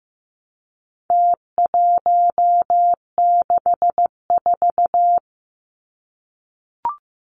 Morse code, heard as a keyed tone, about 15 wpm, 700 Hz. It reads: T164
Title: Numbers - Contest Style - 4-Digit 15wpm